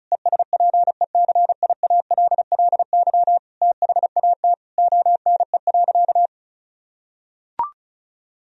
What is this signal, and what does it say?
Morse code, heard as a keyed tone, about 36 wpm, 700 Hz. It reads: ESPECIALLY THAT ONE.
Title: Sentences from Top 1000 Words - Review Part 4 - 35wpm